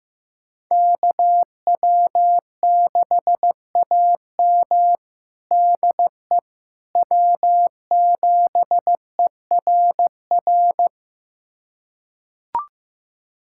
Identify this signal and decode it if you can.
Morse code, heard as a keyed tone, about 15 wpm, 700 Hz. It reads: KW6AM DE W7ERR
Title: USA Callsign DE Callsign - 5 Characters - 15wpm